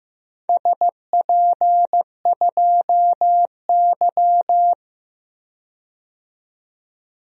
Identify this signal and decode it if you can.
Morse code, heard as a keyed tone, about 16 wpm, 700 Hz. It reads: SP2Y
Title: DXCC-100 Call Signs - 4 Characters - 15wpm